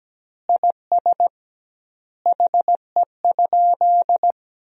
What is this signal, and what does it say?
Morse code, heard as a keyed tone, about 17 wpm, 700 Hz. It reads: IS HE?
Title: 2x Word-Spacing - Sentences from Top 100 Words - 17wpm